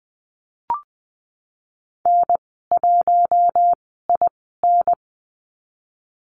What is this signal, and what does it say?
Morse code, heard as a keyed tone, about 20 wpm, spaced as for 15 wpm, 700 Hz. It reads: N1IN
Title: 069 Lesson 12g - Letter d - Call Signs 20wpm